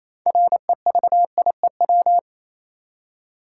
Morse code, heard as a keyed tone, about 28 wpm, 700 Hz. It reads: REVIEW
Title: Six Letter Words - 28wpm